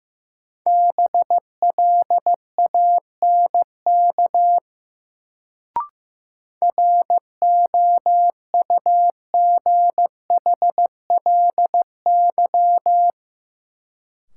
Morse code, heard as a keyed tone, about 16 wpm, 700 Hz. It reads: BLANK ROUGHLY
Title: Sentences from Top 1500 Words - New Word Drill Alt - 15wpm